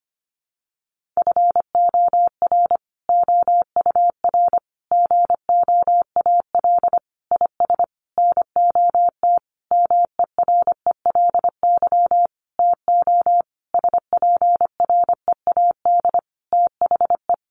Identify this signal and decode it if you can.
Morse code, heard as a keyed tone, about 25 wpm, 700 Hz. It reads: FOR OUR GOAL IS NOT MERELY TO SPREAD THE
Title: Sentences from Top 1400 Words 25wpm